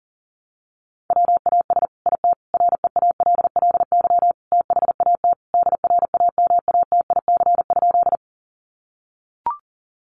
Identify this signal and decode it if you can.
Morse code, heard as a keyed tone, about 40 wpm, 700 Hz. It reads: WAS IT REALLY THAT DRAMATIC?
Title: Sentences from Top 1300 Words 40wpm